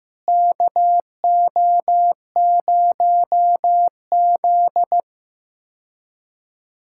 Morse code, heard as a keyed tone, about 15 wpm, 700 Hz. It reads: KO0Z
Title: K1USN SST QSO Practice - 15wpm